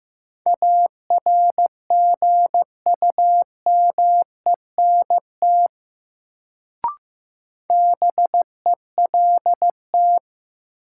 Morse code, heard as a keyed tone, about 15 wpm, 700 Hz. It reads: ARGUMENT BELT
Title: Sentences from Top 1400 Words - New Word Drill - 15wpm